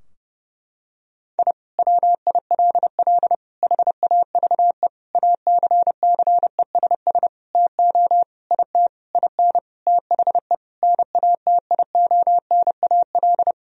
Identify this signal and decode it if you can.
Morse code, heard as a keyed tone, about 30 wpm, 700 Hz. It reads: I WILL HAVE ACCESS TO IT IN THE NATIONAL
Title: Sentences from Top 1000 Words - Review Part 4 - 30wpm